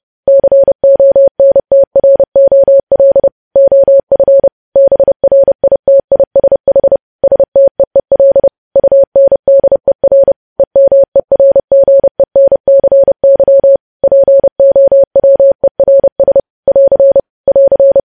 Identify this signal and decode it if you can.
Morse code, heard as a keyed tone, about 30 wpm, 550 Hz. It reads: CONTROL OF BRITISH STEEL UNDER EMERGENCY POWERS <AR> <AR>
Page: news headlines in morse code at 30 wpm